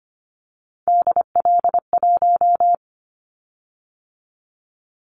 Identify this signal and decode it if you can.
Morse code, heard as a keyed tone, about 25 wpm, 700 Hz. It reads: DL1
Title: Callsign Prefixes for Top 25 DXCC - 25wpm